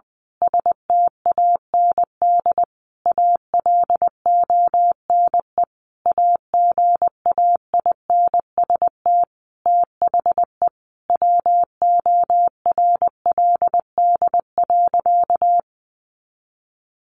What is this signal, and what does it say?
Morse code, heard as a keyed tone, about 19 wpm, 700 Hz. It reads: STAND ALONE AGAINST THE WORLD.
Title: Sentences from Top 1400 Words 20wpm